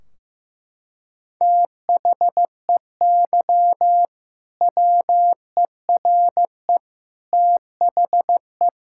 Morse code, heard as a keyed tone, about 16 wpm, 700 Hz. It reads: THEY WERE THE
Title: Sentences from Top 1600 Words 15wpm